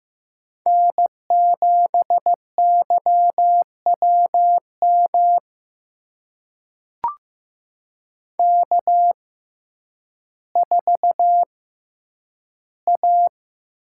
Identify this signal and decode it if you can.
Morse code, heard as a keyed tone, about 15 wpm, spaced as for 9 wpm, 700 Hz. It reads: N7YWM K4A
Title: USA Call Signs - 5 Characters - 6x Character Spacing - 15wpm